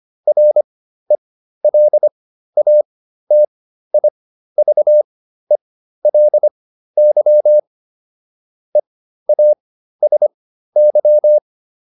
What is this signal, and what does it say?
Morse code, heard as a keyed tone, about 25 wpm, spaced as for 13 wpm, 600 Hz. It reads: RELATIVELY EASY